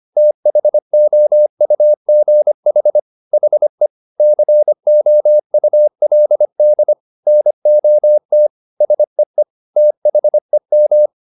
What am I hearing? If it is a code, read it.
Morse code, heard as a keyed tone, about 25 wpm, 600 Hz. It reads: THOUGH HE COULD NOT SEE THEM
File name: the old man and the sea part 1 25wpm